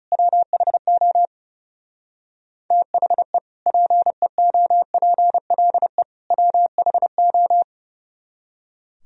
Morse code, heard as a keyed tone, about 39 wpm, 700 Hz. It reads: WHO THE PEOPLE WHO